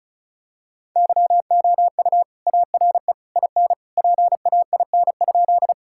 Morse code, heard as a keyed tone, about 36 wpm, 700 Hz. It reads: YOU ARE IN PAIN?
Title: Sentences from Top 1600 Words 35wpm